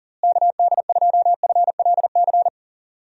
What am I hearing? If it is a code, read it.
Morse code, heard as a keyed tone, about 37 wpm, 700 Hz. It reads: KD2FLC
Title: USA Call Signs - 6 Characters - Repeated 2x - 40wpm